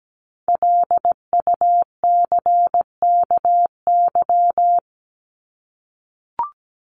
Morse code, heard as a keyed tone, about 17 wpm, 700 Hz. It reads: LUCKY
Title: Top 1751-2000 Words 17wpm